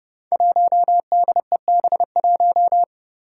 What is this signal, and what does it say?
Morse code, heard as a keyed tone, about 30 wpm, 700 Hz. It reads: 1DEB1
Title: Five Challenging Letter-Number Combinations 30wpm